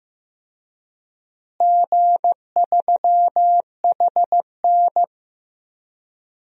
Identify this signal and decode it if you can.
Morse code, heard as a keyed tone, about 15 wpm, 700 Hz. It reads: G3HN
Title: DXCC-100 Call Signs - 4 Characters - Repeated 3x - 15wpm